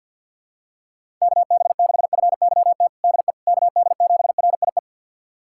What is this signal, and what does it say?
Morse code, heard as a keyed tone, about 50 wpm, 690 Hz. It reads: KB6FYT DE KD7RIE
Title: USA Callsign DE Callsign - 6 Characters - 50wpm